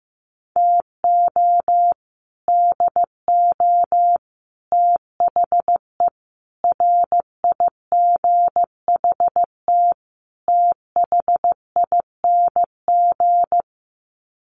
Morse code, heard as a keyed tone, about 14 wpm, 700 Hz. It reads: TO DO THE RIGHT THING
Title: Sets of 5 Words 15wpm